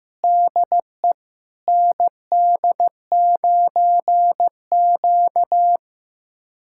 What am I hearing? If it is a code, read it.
Morse code, heard as a keyed tone, about 15 wpm, 700 Hz. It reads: DE ND9Q
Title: USA Callsign DE Callsign - 4 Characters - 15wpm